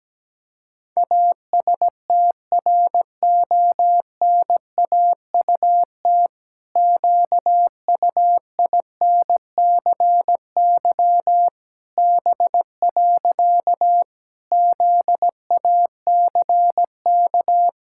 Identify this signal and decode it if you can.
Morse code, heard as a keyed tone, about 17 wpm, 700 Hz. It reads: ASTRONAUT QUINCY B. ZACK
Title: Pangram Challenge v2 - 17wpm